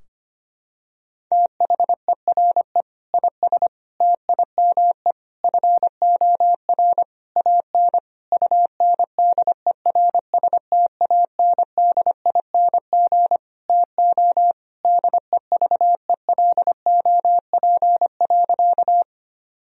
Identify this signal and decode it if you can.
Morse code, heard as a keyed tone, about 25 wpm, 700 Hz. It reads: THERE IS TIME FOR AN UNDERSTANDING TO DEVELOP.
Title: Sentences from Top 1000 Words - Review Part 3 - 25wpm